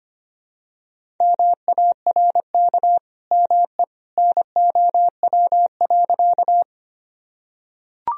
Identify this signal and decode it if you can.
Morse code, heard as a keyed tone, about 25 wpm, 700 Hz. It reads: MARK ME NOW.
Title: Sentences from Top 1000 Words - Review Part 1 - 25wpm